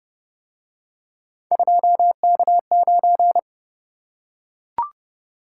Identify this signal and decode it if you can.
Morse code, heard as a keyed tone, about 29 wpm, 700 Hz. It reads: 2K9